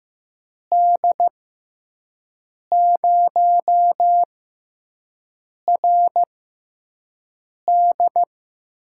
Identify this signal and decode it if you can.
Morse code, heard as a keyed tone, about 16 wpm, spaced as for 5 wpm, 700 Hz. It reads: D0RD
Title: USA Call Signs - 6 Characters - 6x Character Spacing - 15wpm